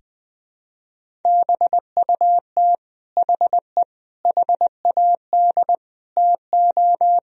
Morse code, heard as a keyed tone, about 20 wpm, 700 Hz. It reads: BUT HE HAD TO
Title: Sentences from Top 100 Words and Abbreviations 20wpm